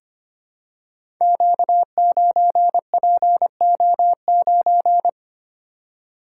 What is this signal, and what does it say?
Morse code, heard as a keyed tone, about 25 wpm, 700 Hz. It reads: Q9PO9
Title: Code Groups - 5 Letter:Numbers - 25wpm